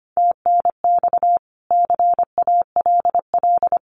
Morse code, heard as a keyed tone, about 25 wpm, 700 Hz. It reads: TNX CALL
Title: Common QSO Phrases 25wpm